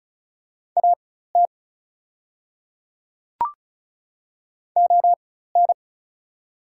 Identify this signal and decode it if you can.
Morse code, heard as a keyed tone, about 36 wpm, spaced as for 16 wpm, 700 Hz. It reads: AT ON
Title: 4x Character Spacing - Top 200 Words - 35wpm